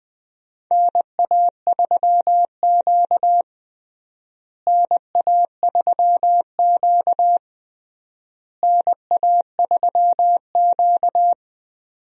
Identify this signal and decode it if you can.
Morse code, heard as a keyed tone, about 20 wpm, 690 Hz. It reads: NA3Q NA3Q NA3Q
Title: USA Call Signs - 4 Characters - Repeated 3x - 20wpm